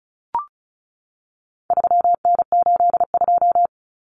Morse code, heard as a keyed tone, about 34 wpm, 700 Hz. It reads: 3N82
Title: Common Cut Numbers - 4-Digit 35wpm